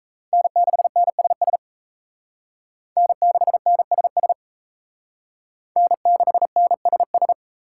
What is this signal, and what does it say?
Morse code, heard as a keyed tone, about 36 wpm, 700 Hz. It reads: N6NSS N6NSS N6NSS